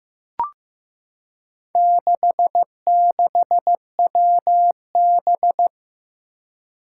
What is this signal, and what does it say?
Morse code, heard as a keyed tone, about 15 wpm, 700 Hz. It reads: 66WB
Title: Four Challenging Letter-Number Combinations 15wpm